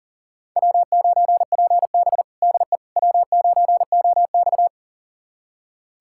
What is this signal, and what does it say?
Morse code, heard as a keyed tone, about 40 wpm, 700 Hz. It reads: W9PB DE W9OX
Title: USA Callsign DE Callsign - 4 Characters - 40wpm